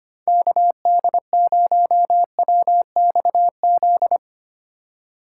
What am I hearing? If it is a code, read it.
Morse code, heard as a keyed tone, about 25 wpm, 700 Hz. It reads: KD0WXZ
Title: USA Call Signs - 6 Characters - 4x Character Spacing - 25wpm